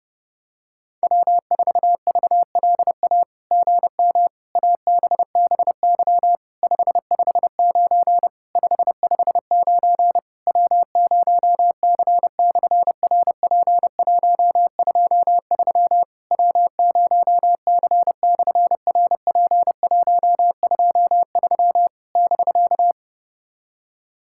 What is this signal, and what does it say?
Morse code, heard as a keyed tone, about 30 wpm, 700 Hz. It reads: W4VLA GM ABBY 559 559 W0C/RP123 W0C/RP123 <BK>
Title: SOTA QSO Practice 30wpm